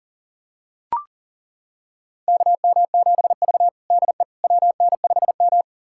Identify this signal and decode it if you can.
Morse code, heard as a keyed tone, about 39 wpm, 700 Hz. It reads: KM7V DE WN5M
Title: USA Callsign DE Callsign - 4 Characters - 40wpm